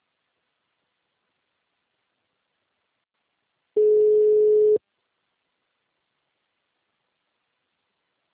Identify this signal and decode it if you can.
Ringback tone (Europe, 425 Hz).